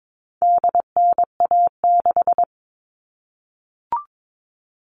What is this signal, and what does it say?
Morse code, heard as a keyed tone, about 22 wpm, 700 Hz. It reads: DNA6
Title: All Cut Numbers - 4-Digit 22wpm